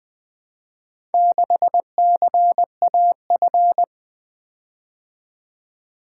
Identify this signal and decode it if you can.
Morse code, heard as a keyed tone, about 24 wpm, 700 Hz. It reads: 6CAF
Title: Speed-Racing - Four Letter-Number Combinations - 20wpm